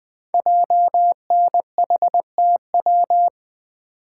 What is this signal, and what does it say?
Morse code, heard as a keyed tone, about 20 wpm, 700 Hz. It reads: JNHTW
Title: Code Groups - 5 Letters - 20wpm